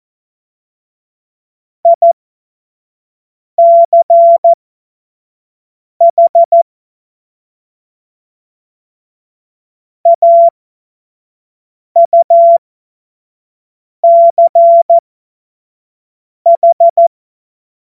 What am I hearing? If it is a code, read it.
Morse code, heard as a keyed tone, about 14 wpm, spaced as for 5 wpm, 680 Hz. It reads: ICH AUCH
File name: Morsekurs / Witze / Witz 09. November 2021, 25 F70